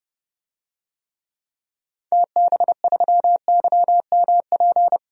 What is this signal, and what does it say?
Morse code, heard as a keyed tone, about 30 wpm, 690 Hz. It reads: TB3YMP
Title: DXCC-100 Call Signs - 6 Characters - 30wpm